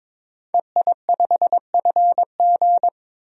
Morse code, heard as a keyed tone, about 22 wpm, 700 Hz. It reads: EI5FG
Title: DXCC-100 Call Signs - 5 Characters - 22wpm